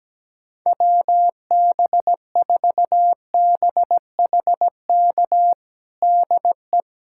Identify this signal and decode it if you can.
Morse code, heard as a keyed tone, about 17 wpm, 700 Hz. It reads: WB4BHK DE
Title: USA Callsign DE Callsign - 6 Characters - 17wpm